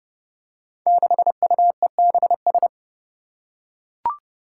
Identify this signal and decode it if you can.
Morse code, heard as a keyed tone, about 30 wpm, 700 Hz. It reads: 6UEBS
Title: Five Challenging Letter-Number Combinations 30wpm